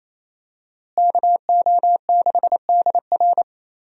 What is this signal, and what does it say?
Morse code, heard as a keyed tone, about 28 wpm, 700 Hz. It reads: KO6DR
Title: USA Call Signs - 5 Characters - Repeated 2x - 28wpm